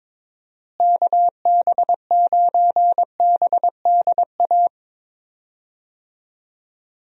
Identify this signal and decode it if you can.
Morse code, heard as a keyed tone, about 22 wpm, 700 Hz. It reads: KB9BDA
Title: USA Call Signs - 6 Characters - Repeated 2x - 22wpm